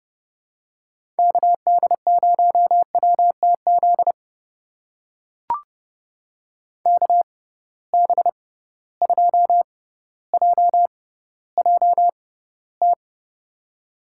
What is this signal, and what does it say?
Morse code, heard as a keyed tone, about 30 wpm, spaced as for 15 wpm, 700 Hz. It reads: KD0WTZ KB2JJT
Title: USA Call Signs - 6 Characters - 6x Character Spacing - 30wpm